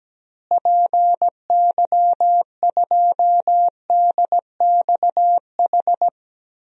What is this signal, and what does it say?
Morse code, heard as a keyed tone, about 17 wpm, 700 Hz. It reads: PY2DXH
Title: DXCC-100 Call Signs - 6 Characters - Repeated 3x - 17wpm